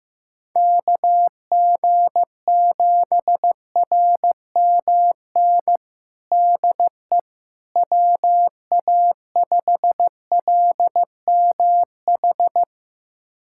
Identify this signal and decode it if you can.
Morse code, heard as a keyed tone, about 15 wpm, 700 Hz. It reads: KG7RMN DE WA5LMH
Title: USA Callsign DE Callsign - 6 Characters - 15wpm